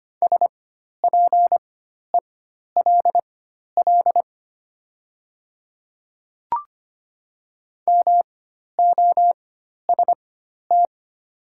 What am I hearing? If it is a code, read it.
Morse code, heard as a keyed tone, about 25 wpm, spaced as for 12 wpm, 690 Hz. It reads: SPELL MOST
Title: 4x Character Spacing - Top 300 Words - 25wpm